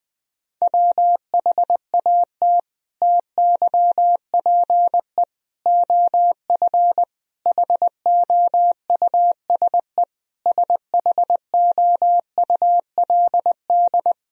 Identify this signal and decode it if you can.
Morse code, heard as a keyed tone, about 21 wpm, 700 Hz. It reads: WHAT TYPE OF HOUSE SHOULD
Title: Sentences from Top 1000 Words - Review Part 3 - 20wpm